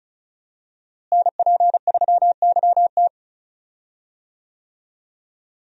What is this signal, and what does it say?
Morse code, heard as a keyed tone, about 36 wpm, 700 Hz. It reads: NP3YT
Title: USA Call Signs - 5 Characters - Repeated 2x - 35wpm